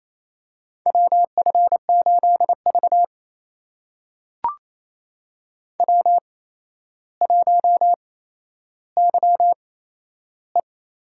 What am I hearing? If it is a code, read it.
Morse code, heard as a keyed tone, about 28 wpm, spaced as for 12 wpm, 700 Hz. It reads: WF8V W1YE
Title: USA Call Signs - 4 Characters - 8x Character Spacing - 28wpm